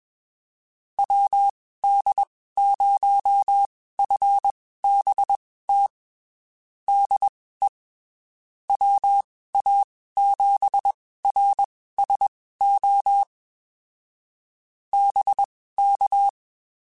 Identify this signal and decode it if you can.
Morse code, heard as a keyed tone, about 22 wpm, spaced as for 15 wpm, 800 Hz. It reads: WD0FBT DE WA7RSO BK
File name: (spaced) / QSOs Spaced-mp3s / WD0FBT 1 @20 sp